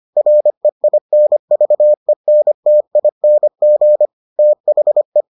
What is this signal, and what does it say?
Morse code, heard as a keyed tone, about 25 wpm, 600 Hz. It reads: REINVENTING THE